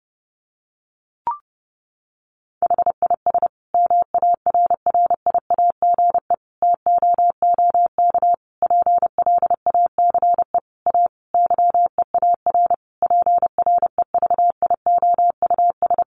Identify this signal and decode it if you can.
Morse code, heard as a keyed tone, about 30 wpm, 700 Hz. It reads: HIS MARRIAGE TOOK PLACE A YEAR PREVIOUS
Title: Sentences from Top 1500 Words 30wpm